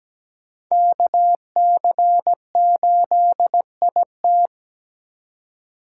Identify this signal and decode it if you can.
Morse code, heard as a keyed tone, about 17 wpm, 700 Hz. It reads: KC8IT
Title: USA Call Signs - 5 Characters - 3x Character Spacing - 17wpm